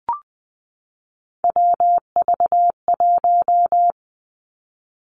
Morse code, heard as a keyed tone, about 19 wpm, 700 Hz. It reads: WV1
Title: Three Challenging Letter-Number Combinations 20wpm